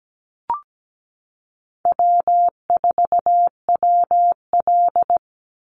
Morse code, heard as a keyed tone, about 17 wpm, 700 Hz. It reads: W4WL
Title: USA Call Signs 17wpm